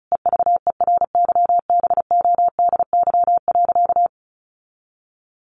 Morse code, heard as a keyed tone, about 34 wpm, 700 Hz. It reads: EVERYBODY.
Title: Sentences from Top 1400 Words 35wpm